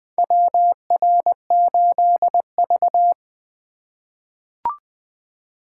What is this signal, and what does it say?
Morse code, heard as a keyed tone, about 21 wpm, 700 Hz. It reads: WR8V
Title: USA Call Signs - 4 Characters - 2x Character Spacing - 20wpm